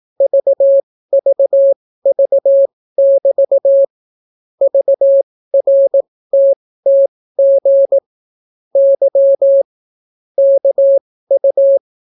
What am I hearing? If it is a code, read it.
Morse code, heard as a keyed tone, about 18 wpm, spaced as for 15 wpm, 550 Hz. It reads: VVV<BT> VRTTG Y KU